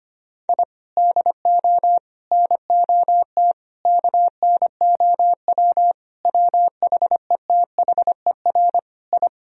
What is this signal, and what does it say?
Morse code, heard as a keyed tone, about 25 wpm, 700 Hz. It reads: I DO NOT KNOW WHETHER I
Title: Sentences from Top 1000 Words - Review Part 3 - 25wpm